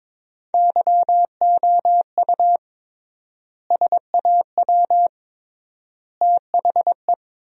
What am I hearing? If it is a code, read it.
Morse code, heard as a keyed tone, about 22 wpm, 700 Hz. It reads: YOU SAW THE
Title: 3x Word-Spacing - Sentences from Top 400 Words - 22wpm